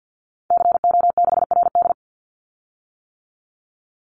Morse code, heard as a keyed tone, about 50 wpm, 700 Hz. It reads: CO6RD
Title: Call Signs - 50wpm